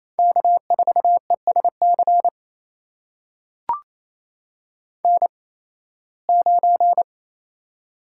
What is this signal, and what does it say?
Morse code, heard as a keyed tone, about 28 wpm, spaced as for 18 wpm, 700 Hz. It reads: K4ESC N9
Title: USA Call Signs - 5 Characters - 8x Character Spacing - 28wpm